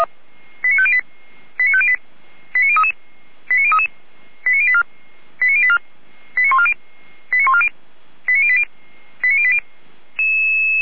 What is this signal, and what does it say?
Five-tone selective call (ZVEI), heard as quick runs of five tones, tones of about 70 ms. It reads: 89508 89508 89033 89033 89084 89084 89140 89140 89080 89080
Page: /455/SDR/ZVEI/89187/